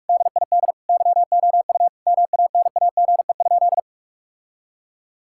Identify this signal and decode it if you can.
Morse code, heard as a keyed tone, about 46 wpm, 700 Hz. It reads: DID YOU MANAGE?
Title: Sentences from Top 1600 Words 45wpm